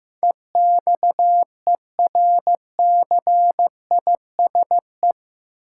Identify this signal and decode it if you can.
Morse code, heard as a keyed tone, about 15 wpm, 700 Hz. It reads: EXERCISE
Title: Sentences from Top 1000 Words - Review Part 4 - New Word Drill - 15wpm